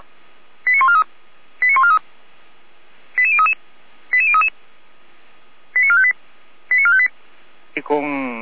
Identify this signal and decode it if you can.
Five-tone selective call (ZVEI), heard as quick runs of five tones, tones of about 70 ms, with speech. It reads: 89143 89143 88033 88033 89458 89458